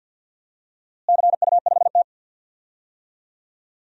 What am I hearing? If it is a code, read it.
Morse code, heard as a keyed tone, about 48 wpm, 700 Hz. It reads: CU5T
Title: DXCC-100 Call Signs - 4 Characters - Repeated 2x - 50wpm